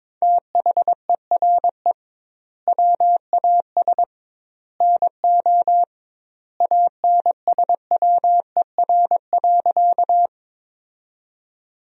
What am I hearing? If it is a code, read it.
Morse code, heard as a keyed tone, about 22 wpm, 700 Hz. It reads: THERE WAS NO ANSWER.
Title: 2x Word-Spacing - Sentences from Top 300 Words - 22wpm